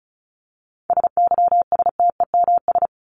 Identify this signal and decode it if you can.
Morse code, heard as a keyed tone, about 34 wpm, 700 Hz. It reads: SYSTEMS
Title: ES:S Suffix - Top 100 Words with Base Words 35wpm